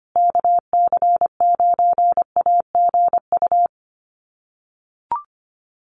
Morse code, heard as a keyed tone, about 25 wpm, 700 Hz. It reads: KC9AGU
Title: USA Call Signs Repeated 2x 25wpm